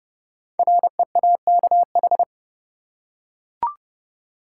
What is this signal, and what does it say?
Morse code, heard as a keyed tone, about 30 wpm, 700 Hz. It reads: REAKH
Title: Code Groups - 5 Letter:Numbers - 30wpm